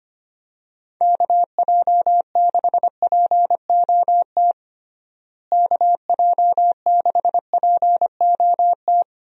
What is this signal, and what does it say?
Morse code, heard as a keyed tone, about 25 wpm, 700 Hz. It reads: KJ6POT KJ6POT
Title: USA Call Signs - 6 Characters - Repeated 2x - 25wpm